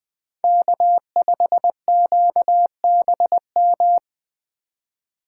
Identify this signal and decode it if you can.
Morse code, heard as a keyed tone, about 21 wpm, 700 Hz. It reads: K5QBM
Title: USA Call Signs - 5 Characters - 3x Character Spacing - 20wpm